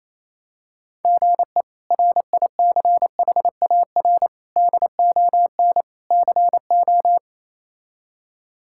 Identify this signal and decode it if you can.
Morse code, heard as a keyed tone, about 28 wpm, 700 Hz. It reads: GE RICHAR DON CO
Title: K1USN SST QSO Practice - 28wpm